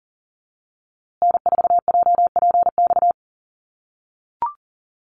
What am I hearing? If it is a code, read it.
Morse code, heard as a keyed tone, about 38 wpm, 700 Hz. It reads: N4JPX
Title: USA Call Signs Repeated 3x 40wpm